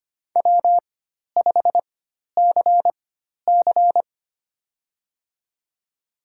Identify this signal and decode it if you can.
Morse code, heard as a keyed tone, about 25 wpm, spaced as for 12 wpm, 700 Hz. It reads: W5CC
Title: USA Call Signs - 4 Characters - 4x Character Spacing - 25wpm